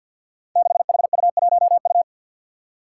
Morse code, heard as a keyed tone, about 52 wpm, 700 Hz. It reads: 6HU1U
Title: Five Challenging Letter-Number Combinations 50wpm